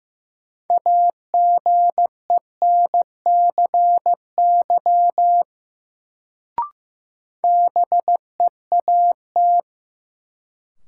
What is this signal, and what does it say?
Morse code, heard as a keyed tone, about 15 wpm, 700 Hz. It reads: AGENCY BEAT
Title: Sentences from Top 1300 Words - New Word Drill Alt - 15wpm